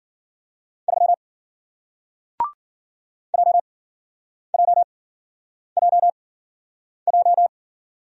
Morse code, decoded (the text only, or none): 3 JJJJ